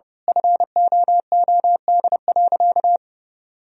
Morse code, heard as a keyed tone, about 30 wpm, 700 Hz. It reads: FOOD.
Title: Sentences from Top 1000 Words - Review Part 1 - 30wpm